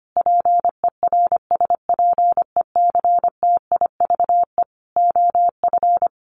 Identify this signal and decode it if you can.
Morse code, heard as a keyed tone, about 25 wpm, 700 Hz. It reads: PERSPECTIVE OF
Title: Sets of 5 Words Encore2 25wpm